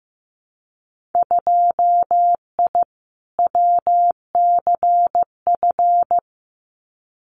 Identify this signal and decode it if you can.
Morse code, heard as a keyed tone, about 15 wpm, 700 Hz. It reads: 2I WCF
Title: Winter Field Day Exchanges 15wpm